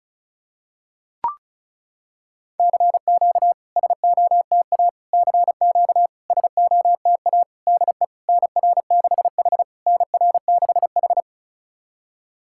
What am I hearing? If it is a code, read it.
Morse code, heard as a keyed tone, about 36 wpm, 690 Hz. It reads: CQ SOTA CQ SOTA DE NR6H NR6H
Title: SOTA QSO Practice 35wpm